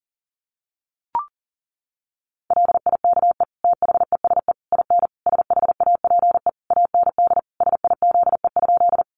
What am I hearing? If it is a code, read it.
Morse code, heard as a keyed tone, about 40 wpm, 700 Hz. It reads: LIKE THESE IN SHAPE AND SIZE?
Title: Sentences from Top 500 Words 40wpm